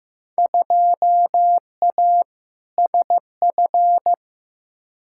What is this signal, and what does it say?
Morse code, heard as a keyed tone, about 15 wpm, 700 Hz. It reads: 2A SF
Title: Field Day Exchanges - 15wpm